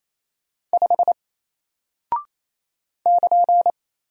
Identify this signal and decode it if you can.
Morse code, heard as a keyed tone, about 28 wpm, 700 Hz. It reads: 5 <KN>
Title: Single Character (Letter:Number:Punctuation:Prosign) 28wpm